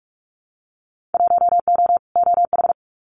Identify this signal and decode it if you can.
Morse code, heard as a keyed tone, about 46 wpm, 700 Hz. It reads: 1O OH
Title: Winter Field Day Exchanges 45wpm